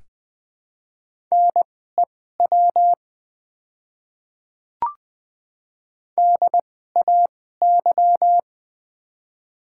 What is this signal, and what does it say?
Morse code, heard as a keyed tone, about 21 wpm, spaced as for 15 wpm, 700 Hz. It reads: NEW DAY
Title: 2x Character Spacing - Top 300 Words - 20wpm